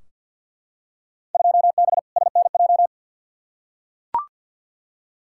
Morse code, heard as a keyed tone, about 52 wpm, 700 Hz. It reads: 2B SNJ
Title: Field Day Exchanges Encore - 50wpm